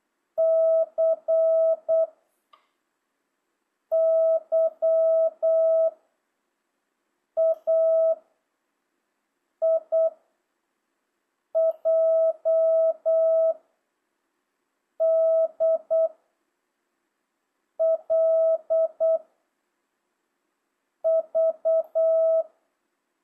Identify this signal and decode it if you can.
Morse code, heard as a keyed tone, about 8 wpm, 660 Hz. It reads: C Y A I J D L V
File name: jh8cbh / 2025cw